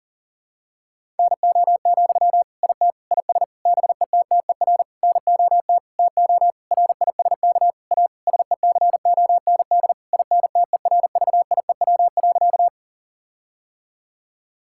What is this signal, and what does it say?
Morse code, heard as a keyed tone, about 37 wpm, 700 Hz. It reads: NO, IT IS BETTER NOT TO RISK A SECOND INTERVIEW.
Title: Sentences from Top 1600 Words 40wpm